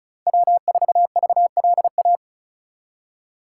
Morse code, heard as a keyed tone, about 36 wpm, 700 Hz. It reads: W4VLA
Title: USA Call Signs - 5 Characters - 4x Character Spacing - 35wpm